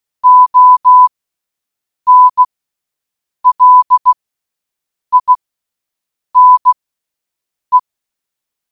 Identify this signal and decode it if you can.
Morse code, heard as a keyed tone, about 16 wpm, spaced as for 7 wpm, 1000 Hz. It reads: ONLINE
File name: Role play group